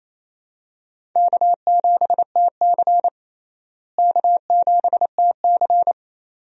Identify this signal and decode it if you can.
Morse code, heard as a keyed tone, about 28 wpm, 700 Hz. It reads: K7TC K7TC
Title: USA Call Signs - 4 Characters - Repeated 2x - 28wpm